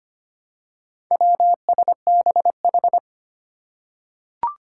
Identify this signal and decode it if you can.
Morse code, heard as a keyed tone, about 25 wpm, 700 Hz. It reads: WSBH
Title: Four Challenging Letter-Number Combinations 25wpm